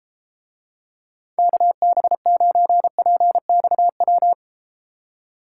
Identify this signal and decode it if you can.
Morse code, heard as a keyed tone, about 33 wpm, 700 Hz. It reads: KB9PXW